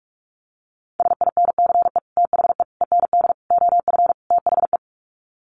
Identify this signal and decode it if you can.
Morse code, heard as a keyed tone, about 44 wpm, 700 Hz. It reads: SINCE THE END OF THE
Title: Sets of 5 Words Encore 45wpm